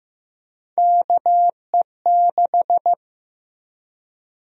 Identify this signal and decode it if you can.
Morse code, heard as a keyed tone, about 15 wpm, 700 Hz. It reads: KE6
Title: USA Call Sign Prefixes - 15wpm